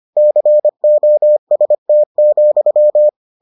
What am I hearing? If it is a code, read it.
Morse code, heard as a keyed tone, about 25 wpm, 600 Hz. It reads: COST,